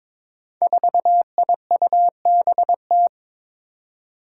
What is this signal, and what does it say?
Morse code, heard as a keyed tone, about 22 wpm, 700 Hz. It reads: 4IUBT
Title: Five Challenging Letter-Number Combinations 22wpm